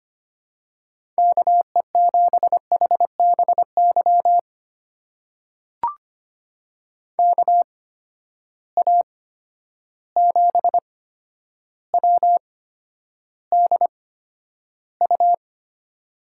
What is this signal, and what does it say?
Morse code, heard as a keyed tone, about 25 wpm, spaced as for 11 wpm, 700 Hz. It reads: KE7HBY KA7WDU
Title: USA Call Signs - 6 Characters - 8x Character Spacing - 25wpm